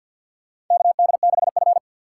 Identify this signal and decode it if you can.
Morse code, heard as a keyed tone, about 48 wpm, 700 Hz. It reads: KD6F
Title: USA Call Signs - 4 Characters - 2x Character Spacing - 50wpm